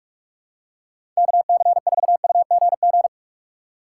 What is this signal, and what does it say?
Morse code, heard as a keyed tone, about 46 wpm, 700 Hz. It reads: KC4UGG
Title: USA Call Signs - 6 Characters - 3x Character Spacing - 45wpm